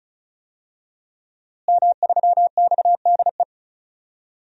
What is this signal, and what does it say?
Morse code, heard as a keyed tone, about 36 wpm, 700 Hz. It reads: M3XDE
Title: DXCC-100 Call Signs - 5 Characters - Repeated 2x - 35wpm